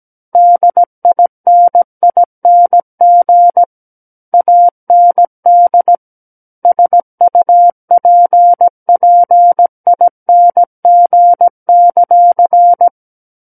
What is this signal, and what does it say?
Morse code, heard as a keyed tone, about 17 wpm, 700 Hz. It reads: DINING AND SUPPING;
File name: Book One / 08-Chapter 8 - 17 WPM